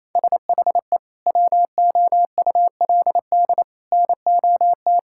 Morse code, heard as a keyed tone, about 28 wpm, 700 Hz. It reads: SHE WOULD NOT
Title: Sentences from Top 1000 Words - Review Part 6 - 28wpm